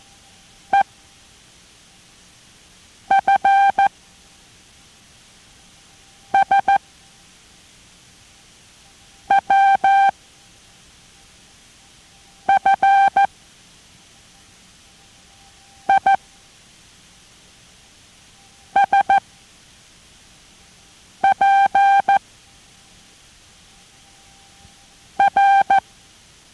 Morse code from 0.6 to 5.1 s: E F